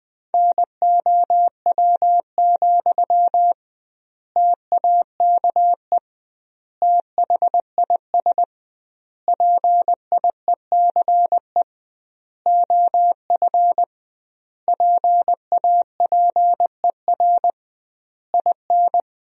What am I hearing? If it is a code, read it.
Morse code, heard as a keyed tone, about 20 wpm, 700 Hz. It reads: NOW, TAKE THIS PIECE OF PAPER IN
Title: 2x Word-Spacing - Sentences from Top 400 Words - 20wpm